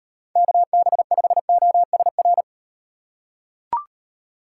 Morse code, heard as a keyed tone, about 38 wpm, 700 Hz. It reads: KB5OSR